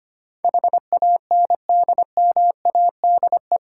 Morse code, heard as a keyed tone, about 25 wpm, 700 Hz. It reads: HANDMADE
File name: practice / 25wpm / Difficult Words with Hints - 25wpm